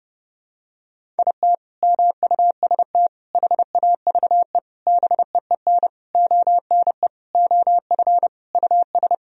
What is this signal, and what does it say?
Morse code, heard as a keyed tone, about 32 wpm, 700 Hz. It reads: IT MUST HAVE BEEN ONE OF US
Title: Sentences from Top 1000 Words - Review Part 1 - 30wpm